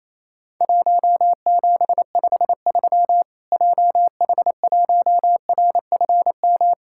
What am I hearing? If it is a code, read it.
Morse code, heard as a keyed tone, about 28 wpm, 700 Hz. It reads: 1753 JH1RFM
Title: Worked All Europe DX Contest - QTC Exchange - 28wpm